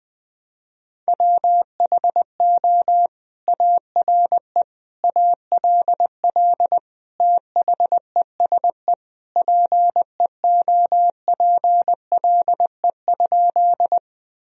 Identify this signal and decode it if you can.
Morse code, heard as a keyed tone, about 20 wpm, 700 Hz. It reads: WHO ARE ALL THESE PEOPLE?
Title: Sentences from Top 1000 Words - Review Part 1 - 20wpm